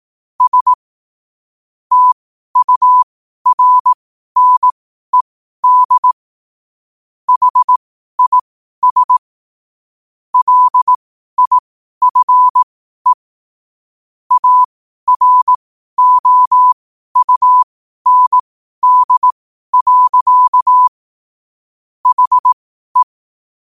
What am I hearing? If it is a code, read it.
Morse code, heard as a keyed tone, about 18 wpm, spaced as for 12 wpm, 1000 Hz. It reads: S TURNED HIS LIFE AROUND. HE
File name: qotdfiles / 12wpm